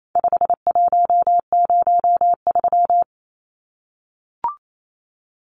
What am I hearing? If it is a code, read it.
Morse code, heard as a keyed tone, about 28 wpm, 700 Hz. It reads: 5103
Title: Numbers - 4-Digit 28wpm